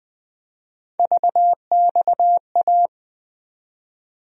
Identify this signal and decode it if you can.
Morse code, heard as a keyed tone, about 20 wpm, 700 Hz. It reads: VXA
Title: USA Call Sign Postfixes - 3 Letters 20wpm